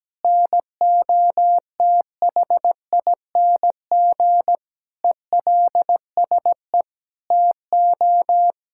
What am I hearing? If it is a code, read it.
Morse code, heard as a keyed tone, about 17 wpm, 700 Hz. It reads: NOTHING ELSE TO
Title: Sentences from Top 1000 Words - Review Part 6 - 17wpm